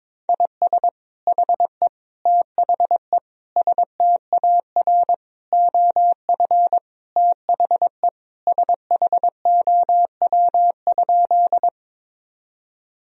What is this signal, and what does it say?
Morse code, heard as a keyed tone, about 22 wpm, 700 Hz. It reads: IS HE THE STAR OF THE SHOW?
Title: Sentences from Top 1000 Words - Review Part 2 - 22wpm